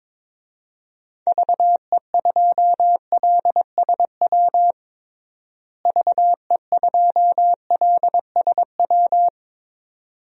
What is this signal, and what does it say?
Morse code, heard as a keyed tone, about 22 wpm, 700 Hz. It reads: VE2LSW VE2LSW
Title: DXCC-100 Call Signs - 6 Characters - Repeated 2x - 22wpm